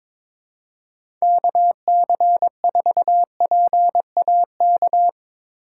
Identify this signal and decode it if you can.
Morse code, heard as a keyed tone, about 22 wpm, 700 Hz. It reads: KC4PAK
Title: USA Call Signs - 6 Characters - 2x Character Spacing - 22wpm